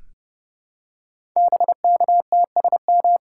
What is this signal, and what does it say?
Morse code, heard as a keyed tone, about 30 wpm, 700 Hz. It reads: BKTSM
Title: Code Groups - 5 Letters - 30wpm